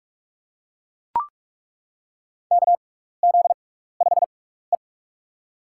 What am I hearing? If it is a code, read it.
Morse code, heard as a keyed tone, about 42 wpm, spaced as for 15 wpm, 700 Hz. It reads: KZ5E
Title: USA Call Signs - 4 Characters - 6x Character Spacing - 45wpm